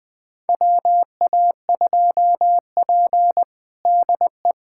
Morse code, heard as a keyed tone, about 21 wpm, 700 Hz. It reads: WA2P DE
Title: USA Callsign DE Callsign - 4 Characters - 20wpm